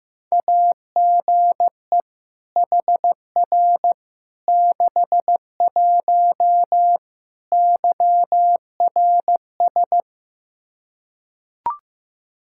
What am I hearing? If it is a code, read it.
Morse code, heard as a keyed tone, about 15 wpm, 700 Hz. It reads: AGE HR 61 YRS
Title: Practice QSO - Exchange 3 - More Info - 15wpm